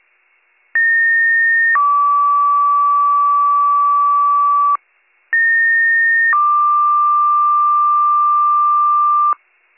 Two-tone page, heard as a one-second tone, then a three-second tone, about 1850 then 1150 Hz, about 1850 then 1180 Hz. It is heard from 0.7 to 9.3 s.